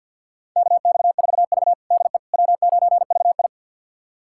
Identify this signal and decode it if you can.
Morse code, heard as a keyed tone, about 49 wpm, 700 Hz. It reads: KX4V DE W9VI
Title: USA Callsign DE Callsign - 4 Characters - 50wpm